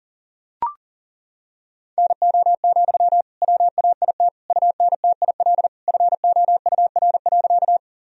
Morse code, heard as a keyed tone, about 40 wpm, 700 Hz. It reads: NO, WAIT UNTIL FOUR.
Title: Sentences from Top 1000 Words - Review Part 1 - 40wpm